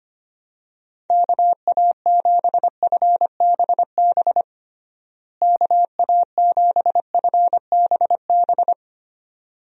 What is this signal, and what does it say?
Morse code, heard as a keyed tone, about 25 wpm, 700 Hz. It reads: KA7FBB KA7FBB
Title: USA Call Signs - 6 Characters - Repeated 2x - 25wpm